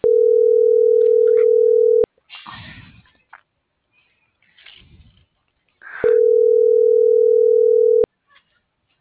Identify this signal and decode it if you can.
Ringback tone (North America), 440 + 480 Hz.